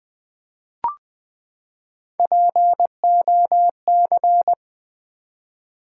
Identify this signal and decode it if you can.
Morse code, heard as a keyed tone, about 20 wpm, 700 Hz. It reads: POC
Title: Build A Word - 3 Letters At A Time - 6-Letter Words - 20wpm